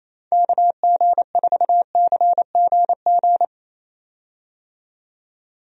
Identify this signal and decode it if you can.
Morse code, heard as a keyed tone, about 27 wpm, 700 Hz. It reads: KG4CGG
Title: USA Call Signs - 6 Characters - 28wpm